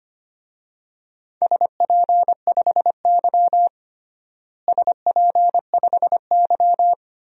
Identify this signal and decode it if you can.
Morse code, heard as a keyed tone, about 25 wpm, 700 Hz. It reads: SP5Y SP5Y
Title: DXCC-100 Call Signs - 4 Characters - Repeated 2x - 25wpm